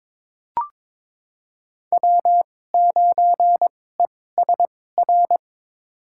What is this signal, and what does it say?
Morse code, heard as a keyed tone, about 22 wpm, spaced as for 16 wpm, 700 Hz. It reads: W9ESR
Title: USA Call Signs - 5 Characters - 2x Character Spacing - 22wpm